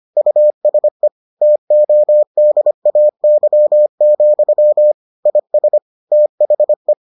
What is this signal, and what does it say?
Morse code, heard as a keyed tone, about 25 wpm, 600 Hz. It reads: USE TODAY, IS THE